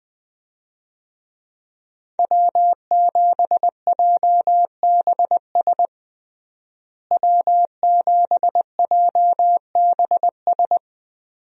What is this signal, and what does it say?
Morse code, heard as a keyed tone, about 20 wpm, 700 Hz. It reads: W7JBS W7JBS
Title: USA Call Signs - 5 Characters - Repeated 2x - 20wpm